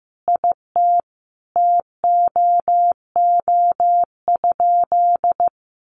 Morse code, heard as a keyed tone, about 15 wpm, 700 Hz. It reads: IT TOO?
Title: Sentences from Top 1100 Words 15wpm